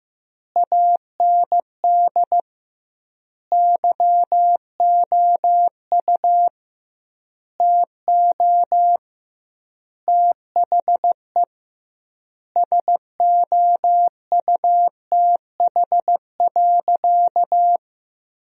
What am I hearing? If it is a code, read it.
Morse code, heard as a keyed tone, about 16 wpm, 700 Hz. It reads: AND YOU TO THE SOUTH.
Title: 2x Word-Spacing - Sentences from Top 400 Words - 15wpm